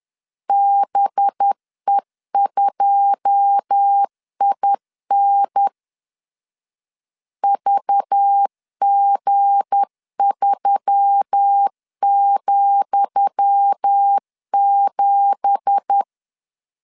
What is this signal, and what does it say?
Morse code, heard as a keyed tone, about 10 wpm, 800 Hz. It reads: BE2IN VG3,7
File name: LEZIONI / 10 SIGNS / 10D COMA 10wpm 3013